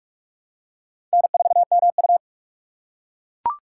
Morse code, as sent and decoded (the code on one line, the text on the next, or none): -. ....- -- ..-
N4MU